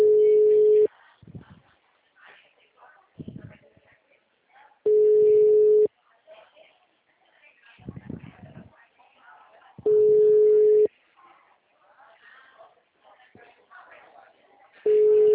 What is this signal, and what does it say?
Ringback tone (Europe, 425 Hz), with speech.